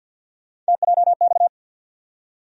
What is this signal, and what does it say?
Morse code, heard as a keyed tone, about 52 wpm, 700 Hz. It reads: TJX
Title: USA Call Sign Postfixes - 3 Letters 50wpm